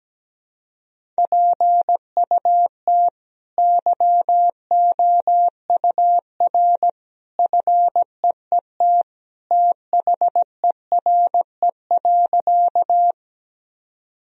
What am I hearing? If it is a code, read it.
Morse code, heard as a keyed tone, about 17 wpm, 700 Hz. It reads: PUT YOUR FEET THERE.
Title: Sentences from Top 1000 Words - Review Part 1 - 17wpm